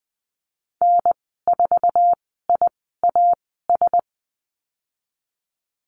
Morse code, heard as a keyed tone, about 20 wpm, spaced as for 14 wpm, 700 Hz. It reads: N4IAS
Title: 069 Lesson 12g - Letter d - Call Signs 20wpm